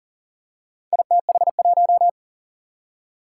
Morse code, heard as a keyed tone, about 42 wpm, 700 Hz. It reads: ITH1
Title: Four Challenging Letter-Number Combinations 40wpm